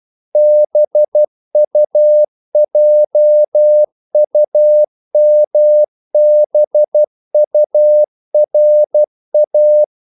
Morse code, heard as a keyed tone, about 12 wpm, 600 Hz. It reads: BUJUMBURA
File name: CW / Practice / NationalCapitals12